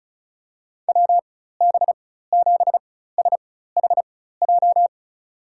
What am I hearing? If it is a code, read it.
Morse code, heard as a keyed tone, about 36 wpm, spaced as for 16 wpm, 700 Hz. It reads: WB7SHJ
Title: USA Call Signs - 6 Characters - 4x Character Spacing - 35wpm